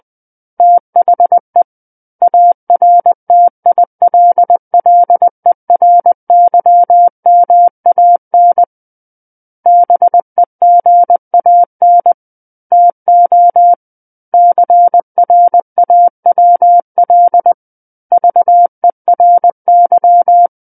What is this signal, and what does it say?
Morse code, heard as a keyed tone, about 20 wpm, 700 Hz. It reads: THE ARTILLERYMAN BEGAN TO CRAWL VERY
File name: 20 WPM